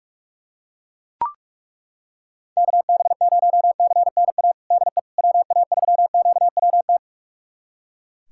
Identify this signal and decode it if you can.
Morse code, heard as a keyed tone, about 41 wpm, 700 Hz. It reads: KB0CNA DE WA3QWT
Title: USA Callsign DE Callsign - 6 Characters - 45wpm